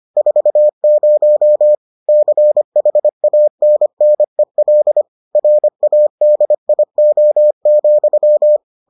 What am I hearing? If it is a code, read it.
Morse code, heard as a keyed tone, about 25 wpm, 600 Hz. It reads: 40 CHANNEL RADIO,